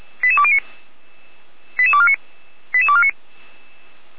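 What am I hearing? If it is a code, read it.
Five-tone selective call (ZVEI), heard as quick runs of five tones, tones of about 70 ms. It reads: 88209 88249 88249
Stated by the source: ZVEI five-tone selective call received on SDR, from the feuerwehr-orlamuende.org (fire brigade Orlamünde) audio server.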